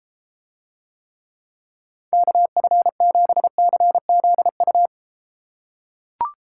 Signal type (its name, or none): Morse code